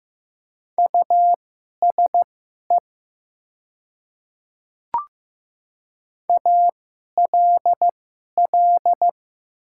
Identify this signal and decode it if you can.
Morse code, heard as a keyed tone, about 15 wpm, spaced as for 11 wpm, 700 Hz. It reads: USE ALL
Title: 2x Character Spacing - Top 200 Words - 15wpm